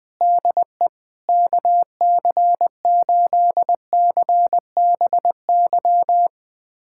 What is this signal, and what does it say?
Morse code, heard as a keyed tone, about 20 wpm, 700 Hz. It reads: DE KC8CBY
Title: USA Callsign DE Callsign 20wpm